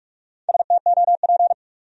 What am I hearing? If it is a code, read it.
Morse code, heard as a keyed tone, about 56 wpm, 700 Hz. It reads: STOP
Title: Speed-Racing Top 300 Words 45wpm